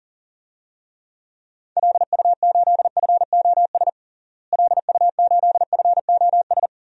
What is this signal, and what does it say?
Morse code, heard as a keyed tone, about 37 wpm, 700 Hz. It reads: LU8FOS LU8FOS
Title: DXCC-100 Call Signs - 6 Characters - Repeated 3x - 40wpm